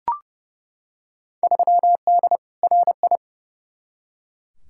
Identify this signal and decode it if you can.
Morse code, heard as a keyed tone, about 30 wpm, 700 Hz. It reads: 3D RI
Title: Field Day Exchanges Encore - 30wpm